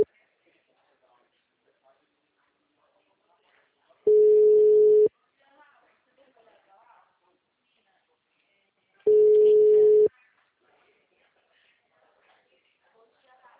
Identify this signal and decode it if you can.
Ringback tone (Europe, 425 Hz), with speech.
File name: out- -2006- - - .20138